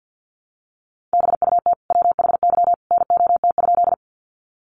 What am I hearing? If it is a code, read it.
Morse code, heard as a keyed tone, about 50 wpm, 700 Hz. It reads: BUT WHY NOT?
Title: Sentences from Top 600 Words 50wpm